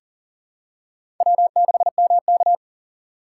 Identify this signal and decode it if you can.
Morse code, heard as a keyed tone, about 40 wpm, 700 Hz. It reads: W6MK
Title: USA Call Signs - 4 Characters - 6x Character Spacing - 40wpm